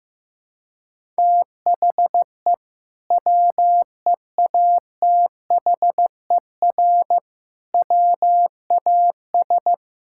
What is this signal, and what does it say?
Morse code, heard as a keyed tone, about 15 wpm, 700 Hz. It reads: THE WEATHER WAS
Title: Sentences from Top 1000 Words - Review Part 5 - 15wpm